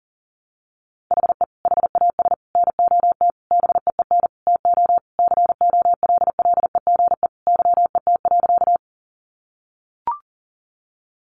Morse code, heard as a keyed tone, about 37 wpm, 700 Hz. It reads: HE HAS NOT BEEN TO COLLEGE YET.
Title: Sentences from Top 1300 Words 40wpm